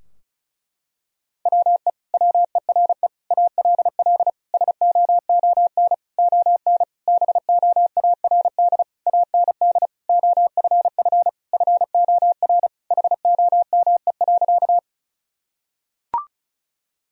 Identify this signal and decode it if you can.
Morse code, heard as a keyed tone, about 34 wpm, 700 Hz. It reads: WE WERE ALL SOON ON BOARD AND OFF FOR HOME.
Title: Sentences from Top 1000 Words - Review Part 4 - 35wpm